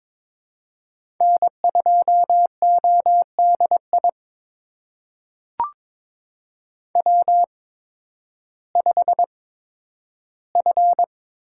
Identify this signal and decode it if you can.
Morse code, heard as a keyed tone, about 22 wpm, spaced as for 12 wpm, 700 Hz. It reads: N2ODI W5F
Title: USA Call Signs - 8x Character Spacing - 22wpm